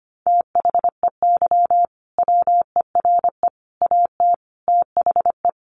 Morse code, heard as a keyed tone, about 25 wpm, 700 Hz. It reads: THEY WERE AT THE
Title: Sentences from Top 1200 Words 25wpm